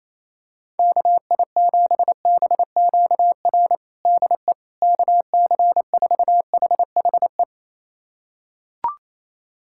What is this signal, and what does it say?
Morse code, heard as a keyed tone, about 27 wpm, 700 Hz. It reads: KI7BQR DE KC4HHE
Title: USA Callsign DE Callsign 28wpm